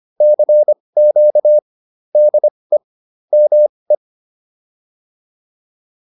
Morse code, heard as a keyed tone, about 25 wpm, spaced as for 20 wpm, 600 Hz. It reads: CQ DE ME
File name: PR103 20